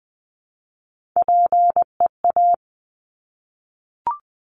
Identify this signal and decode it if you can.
Morse code, heard as a keyed tone, about 20 wpm, 700 Hz. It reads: PEA
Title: Top Three Letter Combinations in Words 20wpm